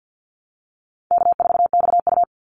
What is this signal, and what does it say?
Morse code, heard as a keyed tone, about 48 wpm, 700 Hz. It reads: K4XU
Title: Easy Call Signs Repeated 2x - 50wpm